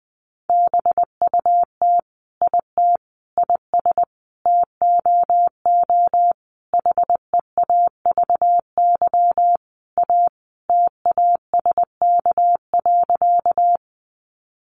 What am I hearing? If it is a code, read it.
Morse code, heard as a keyed tone, about 20 wpm, 700 Hz. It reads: BUT IT IS TOO HEAVY A TASK.
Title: Sentences from Top 1100 Words 20wpm